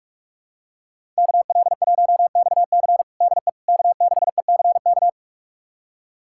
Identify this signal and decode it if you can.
Morse code, heard as a keyed tone, about 41 wpm, 700 Hz. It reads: KL1XC DE K6ECK
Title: USA Callsign DE Callsign - 5 Characters - 45wpm